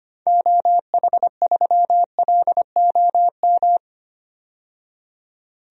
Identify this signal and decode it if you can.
Morse code, heard as a keyed tone, about 25 wpm, 700 Hz. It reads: OH3LOM
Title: DXCC-100 Call Signs - 6 Characters - 25wpm